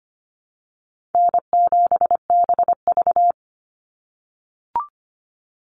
Morse code, heard as a keyed tone, about 25 wpm, 700 Hz. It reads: N7BV
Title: Easy Call Signs - 25wpm